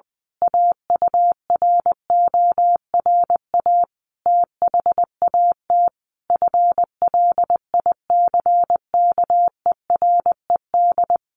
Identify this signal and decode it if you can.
Morse code, heard as a keyed tone, about 20 wpm, 700 Hz. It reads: AURORA THAT FLICKERED
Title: A Sound Of Thunder 20wpm